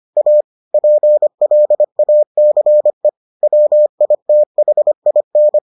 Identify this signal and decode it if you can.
Morse code, heard as a keyed tone, about 25 wpm, 600 Hz. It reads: A PLACE WITHIN